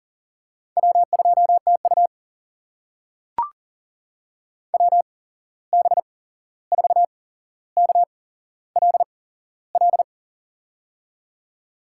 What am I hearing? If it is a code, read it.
Morse code, heard as a keyed tone, about 40 wpm, spaced as for 15 wpm, 700 Hz. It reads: W2TU WB4KLL
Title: USA Call Signs - 8x Character Spacing - 40wpm